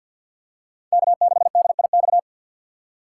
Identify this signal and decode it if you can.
Morse code, heard as a keyed tone, about 52 wpm, 700 Hz. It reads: K6DIX
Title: USA Call Signs - 3x Character Spacing - 50wpm